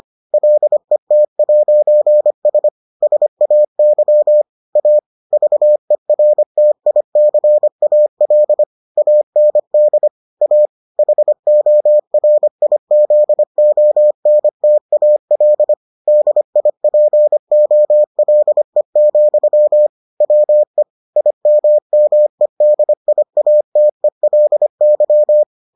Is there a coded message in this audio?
Yes: Morse code from 0.3 to 25.4 s, heard as a keyed tone, about 25 wpm, 600 Hz. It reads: LET'S SAY A VERTICAL AND A HORIZONTAL DIPOLE, WE IMMEDIATELY